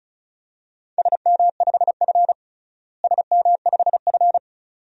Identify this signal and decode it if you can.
Morse code, heard as a keyed tone, about 36 wpm, 700 Hz. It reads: SM5F SM5F
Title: DXCC-100 Call Signs - 4 Characters - Repeated 2x - 35wpm